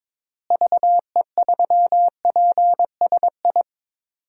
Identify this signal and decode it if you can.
Morse code, heard as a keyed tone, about 22 wpm, 700 Hz. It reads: VE3PSI
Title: DXCC-100 Call Signs - 6 Characters - Repeated 2x - 22wpm